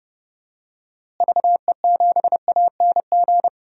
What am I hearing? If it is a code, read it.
Morse code, heard as a keyed tone, about 30 wpm, 700 Hz. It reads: VE7ANG
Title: DXCC-100 Call Signs - 6 Characters - Repeated 3x - 30wpm